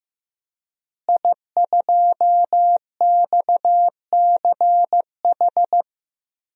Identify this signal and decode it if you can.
Morse code, heard as a keyed tone, about 15 wpm, 700 Hz. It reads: I2XCH
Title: DXCC-100 Call Signs - 5 Characters - Repeated 3x - 15wpm